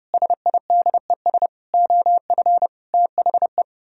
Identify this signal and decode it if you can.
Morse code, heard as a keyed tone, about 29 wpm, 700 Hz. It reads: SIDES OF THE